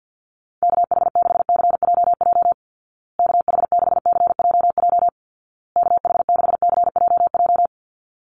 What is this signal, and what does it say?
Morse code, heard as a keyed tone, about 50 wpm, 700 Hz. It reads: KH6CJJ KH6CJJ KH6CJJ
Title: Call Signs Repeated 2x - 50wpm